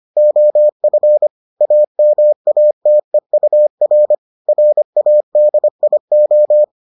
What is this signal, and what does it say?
Morse code, heard as a keyed tone, about 25 wpm, 600 Hz. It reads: OF AMATEUR RADIO